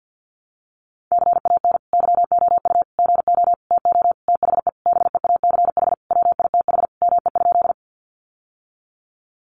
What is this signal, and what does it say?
Morse code, heard as a keyed tone, about 50 wpm, 700 Hz. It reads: CAN YOU GO TO THE BEACH WITH ME?
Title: Sentences from Top 1500 Words 50wpm